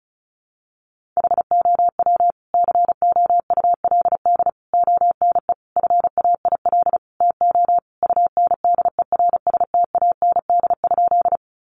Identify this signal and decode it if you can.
Morse code, heard as a keyed tone, about 34 wpm, 700 Hz. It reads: HOW COULD ONE FAIL TO UNDERSTAND?
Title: Sentences from Top 1100 Words 35wpm